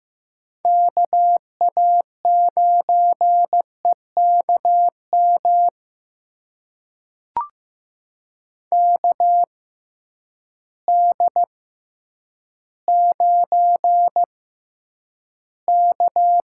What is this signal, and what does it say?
Morse code, heard as a keyed tone, about 15 wpm, spaced as for 9 wpm, 700 Hz. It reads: KA9EKM KD9K
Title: USA Call Signs - 6 Characters - 6x Character Spacing - 15wpm